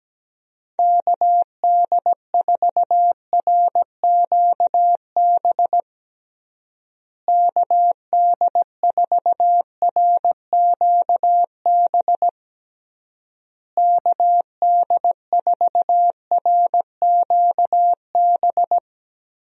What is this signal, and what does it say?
Morse code, heard as a keyed tone, about 17 wpm, 700 Hz. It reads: KD4RQB KD4RQB KD4RQB
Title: USA Call Signs - 6 Characters - Repeated 3x - 17wpm